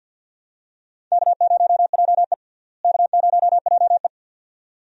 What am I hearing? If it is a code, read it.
Morse code, heard as a keyed tone, about 47 wpm, 690 Hz. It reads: K0JE K0JE
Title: USA Call Signs - 4 Characters - Repeated 2x - 50wpm